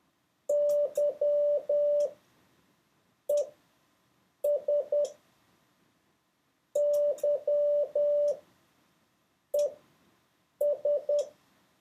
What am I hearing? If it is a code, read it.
Morse code, heard as a keyed tone, about 10 wpm, spaced as for 6 wpm, 580 Hz. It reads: YES YES
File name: 4of6-words-1